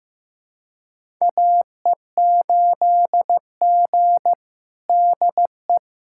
Morse code, heard as a keyed tone, about 16 wpm, 700 Hz. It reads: AE8G DE
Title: USA Callsign DE Callsign 15wpm